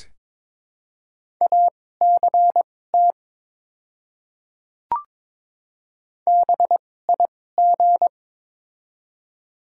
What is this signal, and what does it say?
Morse code, heard as a keyed tone, about 22 wpm, spaced as for 16 wpm, 700 Hz. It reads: ACT BIG
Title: 2x Character Spacing - Top 200 Words - 22wpm